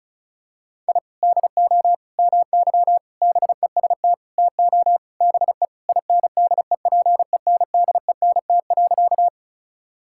Morse code, heard as a keyed tone, about 34 wpm, 700 Hz. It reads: I DO MY BEST TO BE INDEPENDENT.
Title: Sentences from Top 1000 Words - Review Part 6 - 35wpm